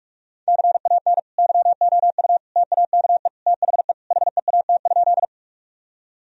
Morse code, heard as a keyed tone, about 46 wpm, 700 Hz. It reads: CAN YOU TAKE THE HEAT?
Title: Sentences from Top 1000 Words - Review Part 3 - 45wpm